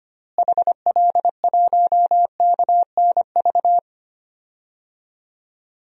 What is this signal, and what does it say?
Morse code, heard as a keyed tone, about 25 wpm, 700 Hz. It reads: HL1KNV
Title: DXCC-100 Call Signs - 6 Characters - Repeated 3x - 25wpm